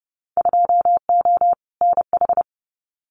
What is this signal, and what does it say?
Morse code, heard as a keyed tone, about 30 wpm, 700 Hz. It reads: 2O NH
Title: Winter Field Day Exchanges 30wpm